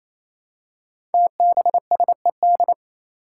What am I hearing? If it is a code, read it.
Morse code, heard as a keyed tone, about 29 wpm, 700 Hz. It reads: TBSED
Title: Five Challenging Letter-Number Combinations 28wpm